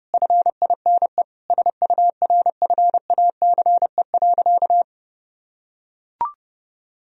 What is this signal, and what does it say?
Morse code, heard as a keyed tone, about 30 wpm, 700 Hz. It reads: FINE SURFACE.
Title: Sentences from Top 1000 Words - Review Part 3 - 30wpm